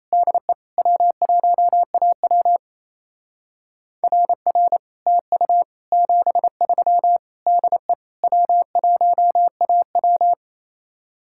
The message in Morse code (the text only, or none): DE W1AW RR TU 73 DE W1AW